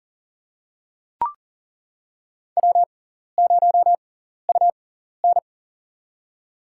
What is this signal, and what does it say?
Morse code, heard as a keyed tone, about 40 wpm, spaced as for 14 wpm, 700 Hz. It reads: W0UN
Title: USA Call Signs - 4 Characters - 6x Character Spacing - 40wpm